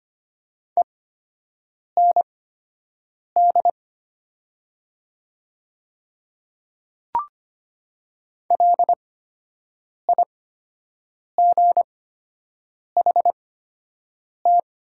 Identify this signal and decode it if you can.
Morse code, heard as a keyed tone, about 25 wpm, spaced as for 7 wpm, 700 Hz. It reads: END LIGHT
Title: 8x Character Spacing - Top 200 Words - 25wpm